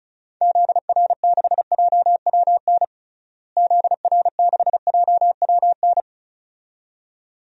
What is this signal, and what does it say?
Morse code, heard as a keyed tone, about 36 wpm, 700 Hz. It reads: ZR6JWN ZR6JWN
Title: DXCC-100 Call Signs - 6 Characters - Repeated 2x - 35wpm